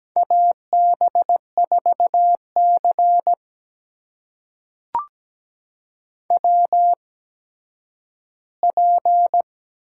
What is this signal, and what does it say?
Morse code, heard as a keyed tone, about 17 wpm, spaced as for 10 wpm, 700 Hz. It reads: AB4C WP
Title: USA Call Signs - 4 Characters - 8x Character Spacing - 17wpm